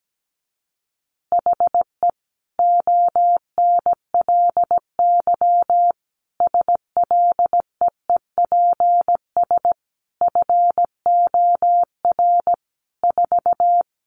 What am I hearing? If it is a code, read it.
Morse code, heard as a keyed tone, about 17 wpm, 700 Hz. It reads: HE ONLY SLEEPS FOR 4
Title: Additional Easy Phrases 17wpm